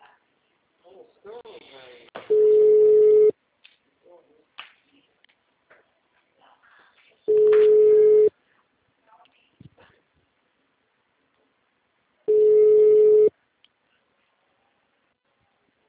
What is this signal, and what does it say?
Ringback tone (Europe, 425 Hz), with speech.